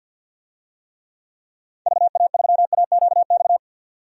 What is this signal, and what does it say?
Morse code, heard as a keyed tone, about 50 wpm, 700 Hz. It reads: VA3AQX
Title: DXCC-100 Call Signs - 6 Characters - 50wpm